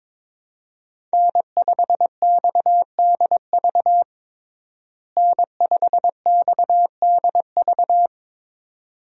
Morse code, heard as a keyed tone, about 21 wpm, 700 Hz. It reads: N5XDV N5XDV
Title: USA Call Signs - 5 Characters - Repeated 2x - 22wpm